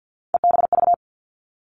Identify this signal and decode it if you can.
Morse code, heard as a keyed tone, about 50 wpm, 700 Hz. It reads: EBV
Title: Three Challenging Letter-Number Combinations 50wpm